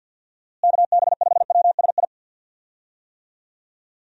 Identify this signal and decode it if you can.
Morse code, heard as a keyed tone, about 49 wpm, 700 Hz. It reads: KB5WSI
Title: USA Call Signs - 6 Characters - Repeated 3x - 50wpm